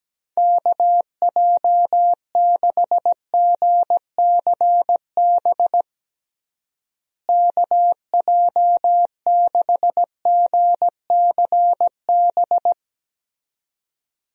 Morse code, heard as a keyed tone, about 17 wpm, 700 Hz. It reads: KJ6GCB KJ6GCB
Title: USA Call Signs - 6 Characters - Repeated 2x - 17wpm